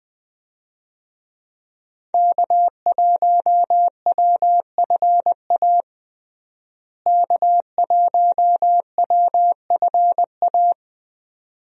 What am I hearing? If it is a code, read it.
Morse code, heard as a keyed tone, about 21 wpm, 700 Hz. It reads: K1WFA K1WFA
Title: USA Call Signs - 5 Characters - Repeated 2x - 20wpm